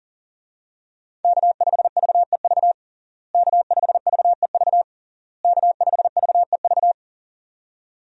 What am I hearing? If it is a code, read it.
Morse code, heard as a keyed tone, about 40 wpm, 700 Hz. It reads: K5VEV K5VEV K5VEV
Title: USA Call Signs - 5 Characters - Repeated 3x - 40wpm